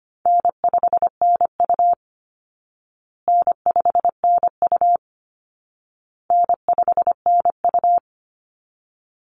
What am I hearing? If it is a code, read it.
Morse code, heard as a keyed tone, about 25 wpm, 700 Hz. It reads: N5NU N5NU N5NU
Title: Easy Call Signs Repeated 3x - 25wpm